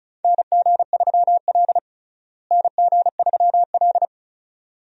Morse code, heard as a keyed tone, about 34 wpm, 700 Hz. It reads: NG3L NG3L
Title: USA Call Signs - 4 Characters - Repeated 2x - 35wpm